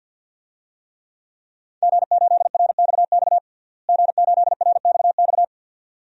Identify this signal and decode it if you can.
Morse code, heard as a keyed tone, about 47 wpm, 700 Hz. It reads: G8RXX G8RXX
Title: DXCC-100 Call Signs - 5 Characters - Repeated 2x - 50wpm